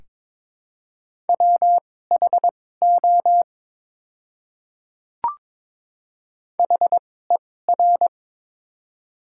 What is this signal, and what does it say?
Morse code, heard as a keyed tone, about 23 wpm, spaced as for 16 wpm, 700 Hz. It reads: WHO HER
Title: 2x Character Spacing - Top 200 Words - 22wpm